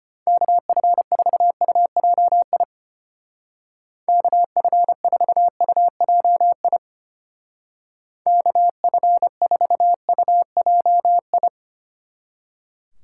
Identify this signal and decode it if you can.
Morse code, heard as a keyed tone, about 30 wpm, 700 Hz. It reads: KF4UJI KF4UJI KF4UJI
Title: Speed Racing - USA 2x3 Call Signs 25wpm